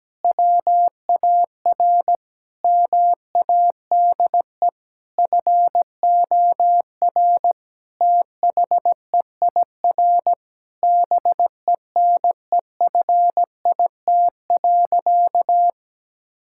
Morse code, heard as a keyed tone, about 17 wpm, 700 Hz. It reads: WAR MADE FOR THEIR BENEFIT.
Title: Sentences from Top 1000 Words - Review Part 6 - 17wpm